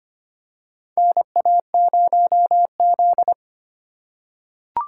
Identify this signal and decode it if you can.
Morse code, heard as a keyed tone, about 25 wpm, 700 Hz. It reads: NA0Z
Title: USA Call Signs - 4 Characters - 3x Character Spacing - 25wpm